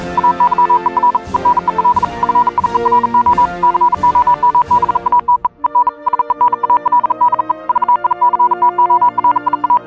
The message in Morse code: AYER EN EL FIERO COMBATE EN SENAL DE VICTORIA